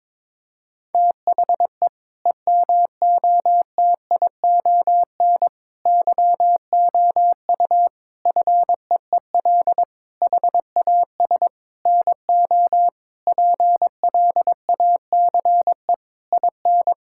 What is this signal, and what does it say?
Morse code, heard as a keyed tone, about 22 wpm, 700 Hz. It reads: THE EMOTION YOU FEEL HAS NO PLACE IN
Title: Sentences from Top 1600 Words 22wpm